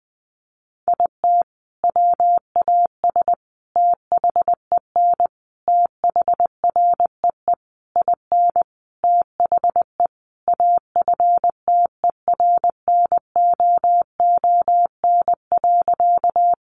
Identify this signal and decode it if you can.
Morse code, heard as a keyed tone, about 20 wpm, 700 Hz. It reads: IT WAS THEN THREE IN THE AFTERNOON.
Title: Sentences from Top 1500 Words 20wpm